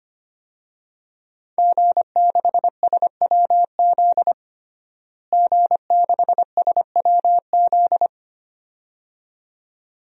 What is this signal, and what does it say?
Morse code, heard as a keyed tone, about 25 wpm, 700 Hz. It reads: G6SWZ G6SWZ
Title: DXCC-100 Call Signs - 5 Characters - Repeated 2x - 25wpm